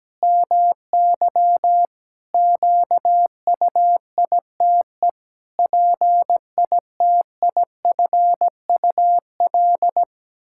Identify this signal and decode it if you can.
Morse code, heard as a keyed tone, about 17 wpm, 700 Hz. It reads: MY QUITE PITIFUL
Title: Pangram Challenge v2 - 17wpm